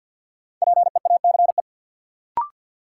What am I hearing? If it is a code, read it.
Morse code, heard as a keyed tone, about 46 wpm, 700 Hz. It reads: PEACE
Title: Sentences from Top 1500 Words - New Word Drill - 50wpm